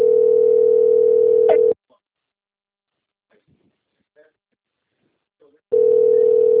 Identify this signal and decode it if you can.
Ringback tone (North America), 440 + 480 Hz.